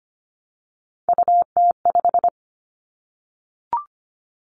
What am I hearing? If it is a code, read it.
Morse code, heard as a keyed tone, about 25 wpm, 700 Hz. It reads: UT5
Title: Callsign Prefixes for Top 25 DXCC - Repeated 3x - 25wpm